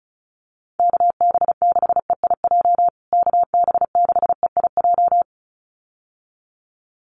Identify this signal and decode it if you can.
Morse code, heard as a keyed tone, about 34 wpm, 700 Hz. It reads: KB6EIJ KB6EIJ
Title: USA Call Signs Repeated 2x 35wpm